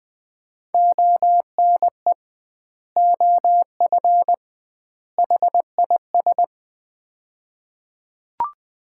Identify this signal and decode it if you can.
Morse code, heard as a keyed tone, about 20 wpm, 700 Hz. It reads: ONE OF HIS
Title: 2x Word-Spacing - Sets of 3 Words - 20wpm